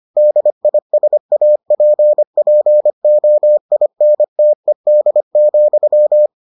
Morse code, heard as a keyed tone, about 25 wpm, 600 Hz. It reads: DISAPPOINTED,